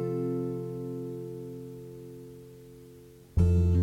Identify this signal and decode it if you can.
Dial tone (North America), 350 + 440 Hz.